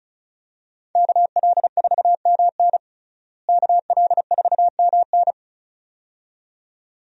Morse code, heard as a keyed tone, about 36 wpm, 700 Hz. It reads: KL4MN KL4MN
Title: USA Call Signs - 5 Characters - Repeated 2x - 35wpm